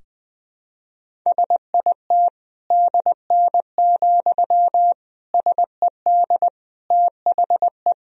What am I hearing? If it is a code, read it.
Morse code, heard as a keyed tone, about 21 wpm, 700 Hz. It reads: SIT DN, SED THE
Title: Sentences from Top 500 Words and Abbreviations 20wpm